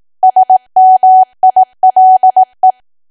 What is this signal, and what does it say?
Morse code, heard as a keyed tone, about 18 wpm, 750 Hz. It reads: SMILE
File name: Archive / 18 WPM / 18WPM